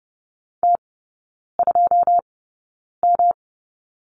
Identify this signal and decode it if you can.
Morse code, heard as a keyed tone, about 30 wpm, spaced as for 9 wpm, 700 Hz. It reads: T2M
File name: Lesson 18e - Letter m - Three Characters 30wpm